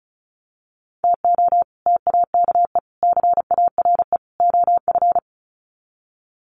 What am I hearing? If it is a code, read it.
Morse code, heard as a keyed tone, about 34 wpm, 700 Hz. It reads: TO TAKE CARE OF
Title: Sets of 4 Words 35wpm